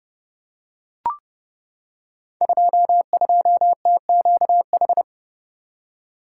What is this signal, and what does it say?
Morse code, heard as a keyed tone, about 30 wpm, 700 Hz. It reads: 22TQH
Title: Code Groups - 5 Letter:Numbers - 30wpm